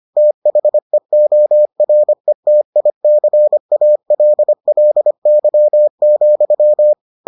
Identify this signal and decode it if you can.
Morse code, heard as a keyed tone, about 25 wpm, 600 Hz. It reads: THEORETICALLY,